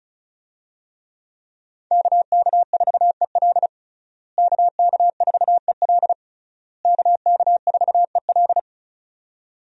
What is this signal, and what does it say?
Morse code, heard as a keyed tone, about 34 wpm, 700 Hz. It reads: KK4EL KK4EL KK4EL
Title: USA Call Signs - 5 Characters - Repeated 3x - 35wpm